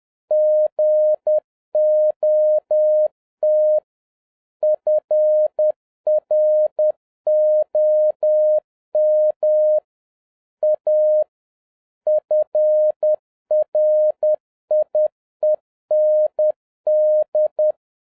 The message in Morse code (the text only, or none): GOT FROM A FRIEND